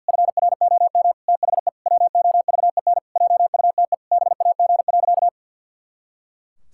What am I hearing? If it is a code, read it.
Morse code, heard as a keyed tone, about 50 wpm, 700 Hz. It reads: FROM THE WOVEN JUTE BAG.
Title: Pangram Challenge v2 - 50wpm